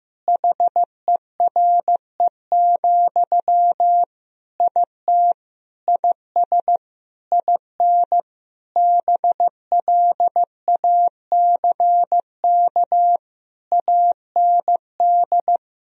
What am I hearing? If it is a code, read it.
Morse code, heard as a keyed tone, about 16 wpm, 700 Hz. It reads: HERE, IT IS IN BLACK AND
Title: Sentences from Top 1000 Words - Review Part 2 - 15wpm